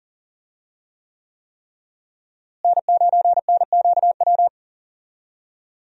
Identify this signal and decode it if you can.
Morse code, heard as a keyed tone, about 40 wpm, 700 Hz. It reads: N9NQW
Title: USA Call Signs - 5 Characters - Repeated 3x - 40wpm